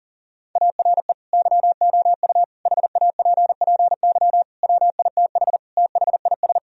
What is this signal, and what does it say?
Morse code, heard as a keyed tone, about 40 wpm, 700 Hz. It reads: ARE YOU HAPPY WITH THIS
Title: Sentences from Top 1000 Words - Review Part 2 - 40wpm